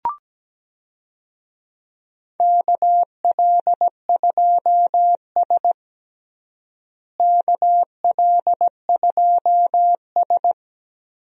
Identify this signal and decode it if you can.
Morse code, heard as a keyed tone, about 17 wpm, 700 Hz. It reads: KL2S KL2S
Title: USA Call Signs - 4 Characters - Repeated 2x - 17wpm